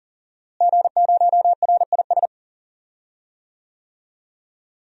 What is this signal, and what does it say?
Morse code, heard as a keyed tone, about 40 wpm, 700 Hz. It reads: G0RIS
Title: DXCC-100 Call Signs - 5 Characters - Repeated 3x - 40wpm